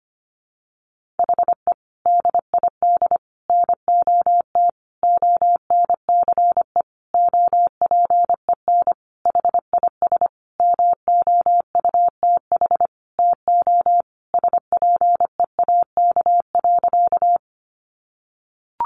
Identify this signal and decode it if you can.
Morse code, heard as a keyed tone, about 25 wpm, 700 Hz. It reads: HE DID NOT ONCE OPEN HIS MOUTH TO SPEAK.
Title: Sentences from Top 1500 Words 25wpm